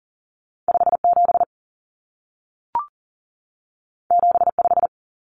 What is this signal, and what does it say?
Morse code, heard as a keyed tone, about 40 wpm, 700 Hz. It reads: 57 75
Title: Common Cut Numbers - 2-Digit 40wpm